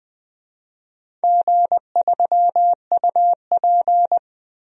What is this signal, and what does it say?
Morse code, heard as a keyed tone, about 20 wpm, 700 Hz. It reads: G3UP
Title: DXCC-100 Call Signs - 4 Characters - Repeated 2x - 20wpm